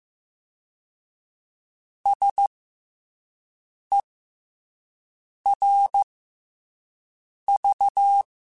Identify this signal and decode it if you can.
Morse code, heard as a keyed tone, about 15 wpm, 800 Hz. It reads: S E R V
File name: Special INGs-mp3s / Special INGs-mp3s / SlowToFaster ing @10